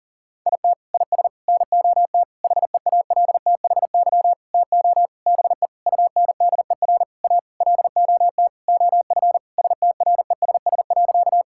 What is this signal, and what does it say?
Morse code, heard as a keyed tone, about 40 wpm, 700 Hz. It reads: IT IS NOT HEALTHY TO BE UNDER A LOT OF STRESS.
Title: Sentences from Top 1000 Words - Review Part 6 - 40wpm